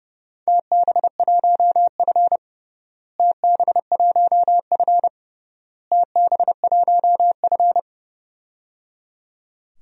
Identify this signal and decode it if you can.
Morse code, heard as a keyed tone, about 30 wpm, 700 Hz. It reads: TB1F TB1F TB1F
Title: DXCC-100 Call Signs - 4 Characters - Repeated 3x - 30wpm